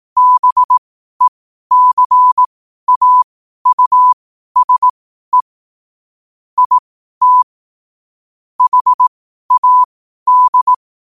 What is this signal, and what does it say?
Morse code, heard as a keyed tone, about 18 wpm, spaced as for 12 wpm, 1000 Hz. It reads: BECAUSE IT HAD
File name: qotdfiles / 12wpm